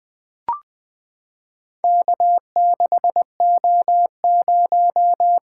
Morse code, heard as a keyed tone, about 20 wpm, 700 Hz. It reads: K6O0
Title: Four Letter-Number Combinations - 20wpm